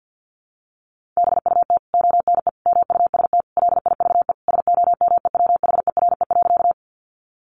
Morse code, heard as a keyed tone, about 50 wpm, 700 Hz. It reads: BUT ONE MUST LIVE SOMEWHERE.
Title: Sentences from Top 1400 Words 50wpm